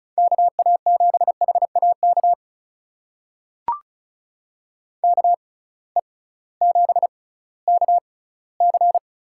Morse code, heard as a keyed tone, about 34 wpm, spaced as for 19 wpm, 700 Hz. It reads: KA7HAK KE7KC
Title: USA Call Signs - 6 Characters - 6x Character Spacing - 35wpm